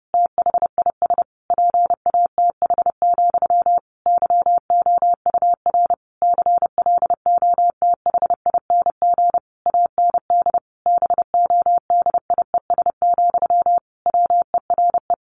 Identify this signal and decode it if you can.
Morse code, heard as a keyed tone, about 30 wpm, 700 Hz. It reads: THIS PATH, YOUR CLOTHING AND BODIES, WERE
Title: A Sound Of Thunder 30wpm